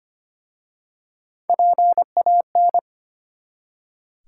Morse code, heard as a keyed tone, about 25 wpm, 700 Hz. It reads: PAN
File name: practice / 25wpm / Easy Words with Hints - 25wpm